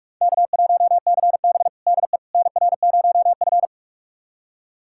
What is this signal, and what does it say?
Morse code, heard as a keyed tone, about 46 wpm, 700 Hz. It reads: K1CB DE NR0F
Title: USA Callsign DE Callsign - 4 Characters - 45wpm